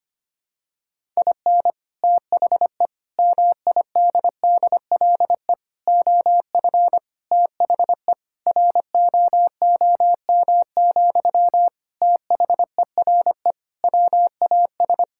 Morse code, heard as a keyed tone, about 25 wpm, 700 Hz. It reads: IN THE MIDDLE OF THE ROOM, THERE WAS
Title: Sentences from Top 1000 Words - Review Part 6 - 25wpm